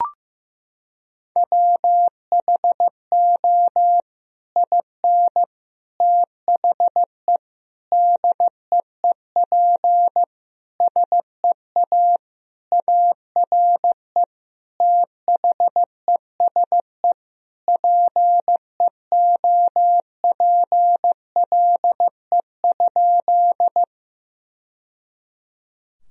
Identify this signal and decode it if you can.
Morse code, heard as a keyed tone, about 15 wpm, 700 Hz. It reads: WHO IN THE DEEP SEA ARE THESE PEOPLE?
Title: Sentences from Top 1000 Words - Review Part 3 - 15wpm